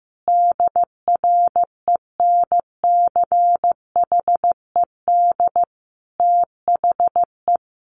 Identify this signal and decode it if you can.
Morse code, heard as a keyed tone, about 15 wpm, 700 Hz. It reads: DRENCHED THE
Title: A Sound Of Thunder 15wpm